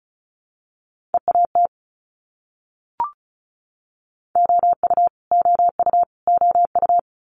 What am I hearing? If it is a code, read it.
Morse code, heard as a keyed tone, about 34 wpm, 700 Hz. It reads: EAT OU OU OU
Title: Multi-letter Phonemes Repeated 3x 35wpm